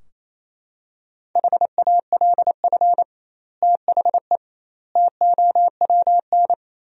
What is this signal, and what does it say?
Morse code, heard as a keyed tone, about 28 wpm, 700 Hz. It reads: HALF THE TOWN
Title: 2x Word-Spacing - Sentences from Top 500 Words - 28wpm